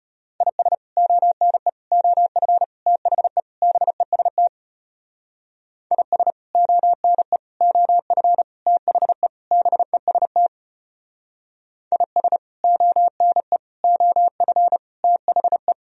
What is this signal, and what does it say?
Morse code, heard as a keyed tone, about 33 wpm, 700 Hz. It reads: IS ONE OF THE BEST IS ONE OF THE BEST IS ONE OF THE